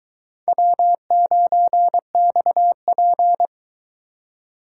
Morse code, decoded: W9XP

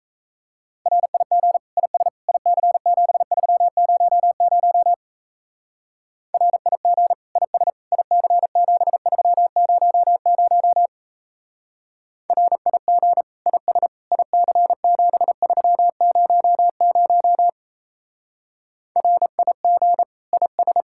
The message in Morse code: RIG IS IC7300 RIG IS IC7300 RIG IS IC7300 RIG IS

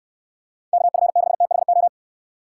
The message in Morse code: KF6THO

